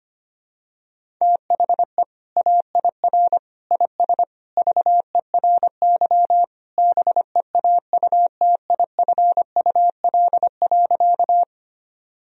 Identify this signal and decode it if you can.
Morse code, heard as a keyed tone, about 25 wpm, 700 Hz. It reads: THE AIR IS VERY BEAUTIFUL.
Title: Sentences from Top 1000 Words - Review Part 1 - 25wpm